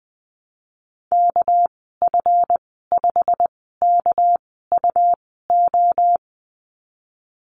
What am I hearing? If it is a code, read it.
Morse code, heard as a keyed tone, about 20 wpm, spaced as for 15 wpm, 700 Hz. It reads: KF5KUO
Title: 213 Lesson 34g - Number 8 - Call Signs 20wpm